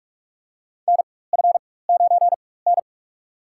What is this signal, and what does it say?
Morse code, heard as a keyed tone, about 46 wpm, spaced as for 21 wpm, 700 Hz. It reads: NF9N
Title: USA Call Signs - 4 Characters - 4x Character Spacing - 45wpm